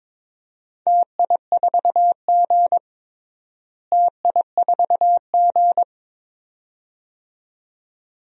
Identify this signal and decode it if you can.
Morse code, heard as a keyed tone, about 22 wpm, 700 Hz. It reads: TI4G TI4G
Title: DXCC-100 Call Signs - 4 Characters - Repeated 2x - 22wpm